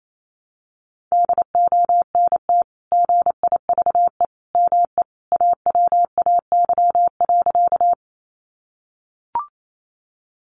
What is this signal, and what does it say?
Morse code, heard as a keyed tone, about 28 wpm, 700 Hz. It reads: DONT GIVE ME AWAY.
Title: Sentences from Top 600 Words 28wpm